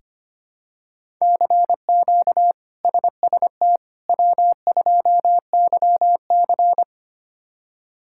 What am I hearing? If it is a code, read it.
Morse code, heard as a keyed tone, about 25 wpm, 700 Hz. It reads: CQ SST W2YC
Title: K1USN SST QSO Practice - 25wpm